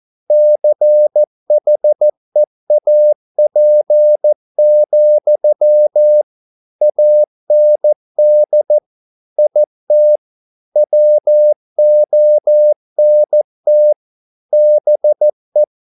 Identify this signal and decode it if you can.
Morse code, heard as a keyed tone, about 14 wpm, 600 Hz. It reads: CHEAP, AND IT WONT BE